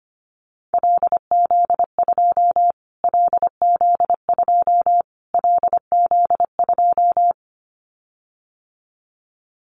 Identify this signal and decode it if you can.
Morse code, heard as a keyed tone, about 25 wpm, 700 Hz. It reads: LZ2 LZ2 LZ2
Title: Callsign Prefixes for Top 25 DXCC - Repeated 3x - 25wpm